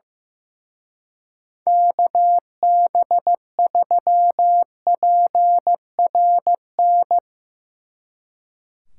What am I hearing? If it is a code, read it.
Morse code, heard as a keyed tone, about 16 wpm, 700 Hz. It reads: KB3PRN
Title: Speed Racing - USA 2x3 Call Signs 15wpm